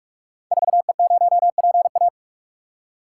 Morse code, heard as a keyed tone, about 44 wpm, 700 Hz. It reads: 4E0PA
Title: Code Groups - 5 Letter:Numbers - 45wpm